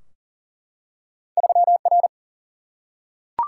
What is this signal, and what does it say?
Morse code, heard as a keyed tone, about 41 wpm, 700 Hz. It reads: 3R
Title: Single-Letter Single-Number 40wpm